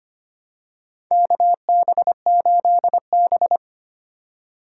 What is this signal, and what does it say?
Morse code, heard as a keyed tone, about 25 wpm, 700 Hz. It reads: KB8B
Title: USA Call Signs - 4 Characters - 3x Character Spacing - 25wpm